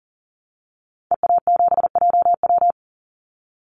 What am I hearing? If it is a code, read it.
Morse code, heard as a keyed tone, about 40 wpm, 700 Hz. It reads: EA7JW
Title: Call Signs Repeated 2x - 40wpm